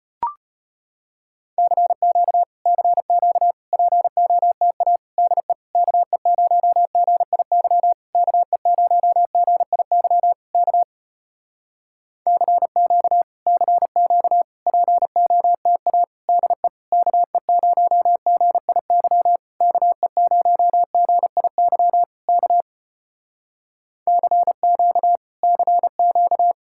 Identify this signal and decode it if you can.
Morse code, heard as a keyed tone, about 34 wpm, 700 Hz. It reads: CQ CQ POTA DE KE0GIY KE0GIY K CQ CQ POTA DE KE0GIY KE0GIY K CQ CQ